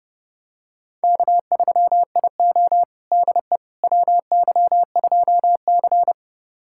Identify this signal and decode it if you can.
Morse code, heard as a keyed tone, about 30 wpm, 700 Hz. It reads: K3IO DE WY2C
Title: USA Callsign DE Callsign - 4 Characters - 30wpm